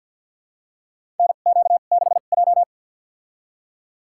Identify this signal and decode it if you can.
Morse code, heard as a keyed tone, about 50 wpm, spaced as for 36 wpm, 700 Hz. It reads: NQ6J
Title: USA Call Signs - 4 Characters - 2x Character Spacing - 50wpm